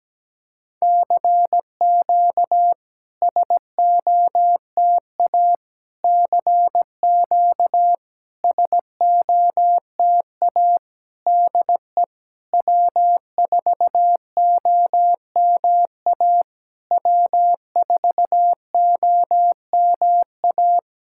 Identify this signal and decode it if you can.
Morse code, heard as a keyed tone, about 17 wpm, 700 Hz. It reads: CQ SOTA CQ SOTA DE W4OMA W4OMA
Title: SOTA QSO Practice 17wpm